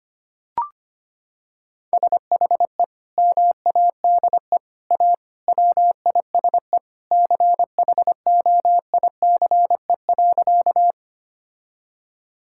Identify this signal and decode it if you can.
Morse code, heard as a keyed tone, about 25 wpm, 700 Hz. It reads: SHE MADE A WISE CHOICE.
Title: Sentences from Top 1600 Words 25wpm